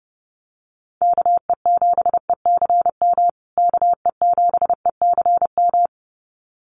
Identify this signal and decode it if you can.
Morse code, heard as a keyed tone, about 30 wpm, 700 Hz. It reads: KE7ECM KE7ECM
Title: USA Call Signs Repeated 2x 30wpm